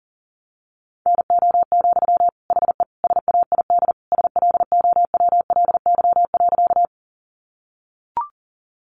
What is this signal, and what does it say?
Morse code, heard as a keyed tone, about 40 wpm, 700 Hz. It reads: NO, HE SAID SLOWLY.
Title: Sentences from Top 1100 Words 40wpm